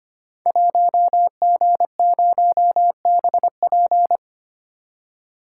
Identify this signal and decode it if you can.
Morse code, heard as a keyed tone, about 25 wpm, 700 Hz. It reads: 1G0BP
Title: Code Groups - 5 Letter:Numbers - 25wpm